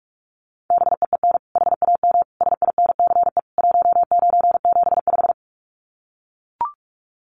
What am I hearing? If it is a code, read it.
Morse code, heard as a keyed tone, about 44 wpm, 700 Hz. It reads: BEEN HAM SINCE 1975
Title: Common QSO Phrases 45wpm